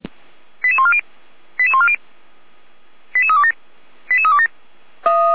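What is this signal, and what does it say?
Five-tone selective call (ZVEI), heard as quick runs of five tones, tones of about 70 ms. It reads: 88140 88140 88327 88327
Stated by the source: ZVEI five-tone selective call 88207 received by SDR at feuerwehr-orlamuende.org, fire brigade.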